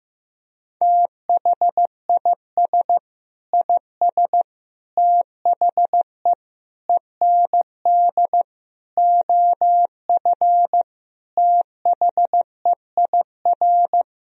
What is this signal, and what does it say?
Morse code, heard as a keyed tone, about 15 wpm, 700 Hz. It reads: THIS IS THE END OF THEIR
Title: Sentences from Top 1000 Words - Review Part 1 - 15wpm